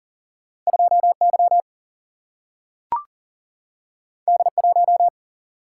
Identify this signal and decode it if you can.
Morse code, heard as a keyed tone, about 39 wpm, 700 Hz. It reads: 2Y D1
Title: Single-Letter Single-Number 40wpm